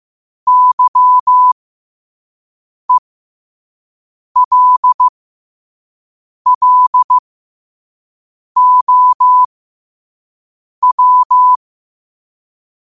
Morse code, heard as a keyed tone, about 15 wpm, spaced as for 5 wpm, 1000 Hz. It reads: YELLOW
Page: New quotes every day in morse code at 5 Words per minute.